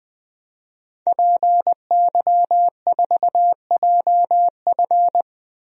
Morse code, heard as a keyed tone, about 21 wpm, 700 Hz. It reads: PY4JF
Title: DXCC-100 Call Signs - 5 Characters - 20wpm